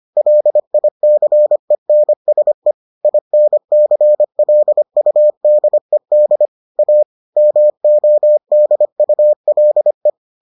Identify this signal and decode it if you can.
Morse code, heard as a keyed tone, about 25 wpm, 600 Hz. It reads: LICENSE INCLUDED A MODULE